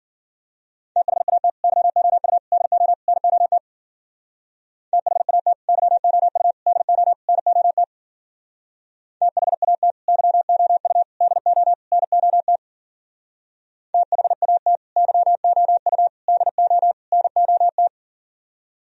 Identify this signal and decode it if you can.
Morse code, heard as a keyed tone, about 49 wpm, 700 Hz. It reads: THAT YOU DO NOT THAT YOU DO NOT THAT YOU DO NOT THAT YOU DO NOT
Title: Speed-Racing - Sets of 4 Words 40wpm